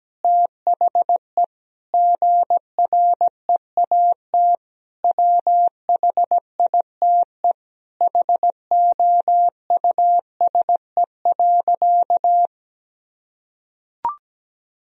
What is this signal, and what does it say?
Morse code, heard as a keyed tone, about 17 wpm, 700 Hz. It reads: THE GREAT WHITE HOUSE.
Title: Sentences from Top 1000 Words - Review Part 1 - 17wpm